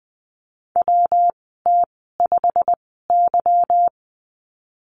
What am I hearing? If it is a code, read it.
Morse code, heard as a keyed tone, about 20 wpm, spaced as for 15 wpm, 700 Hz. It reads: WT5Y
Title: 179 Lesson 29g - Slash - Call Signs 20wpm